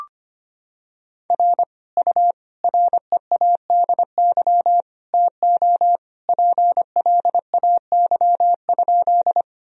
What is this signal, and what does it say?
Morse code, heard as a keyed tone, about 25 wpm, 700 Hz. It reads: R U READY TO PLAY?
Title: Sentences from Top 400 Words and Abbreviations 25wpm